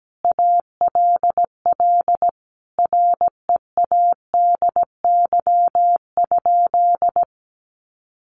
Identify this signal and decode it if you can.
Morse code, heard as a keyed tone, about 17 wpm, 700 Hz. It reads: ALL READY?
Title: Sentences from Top 400 Words 17wpm